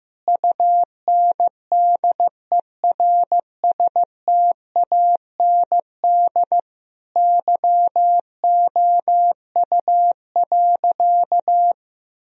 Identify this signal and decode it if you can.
Morse code, heard as a keyed tone, about 16 wpm, 700 Hz. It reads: UNDERSTAND YOU.
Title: Sentences from Top 1000 Words - Review Part 2 - 15wpm